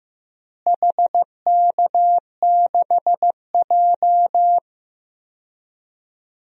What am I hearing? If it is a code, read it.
Morse code, heard as a keyed tone, about 15 wpm, 700 Hz. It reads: HK6J
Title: DXCC-100 Call Signs - 4 Characters - Repeated 3x - 15wpm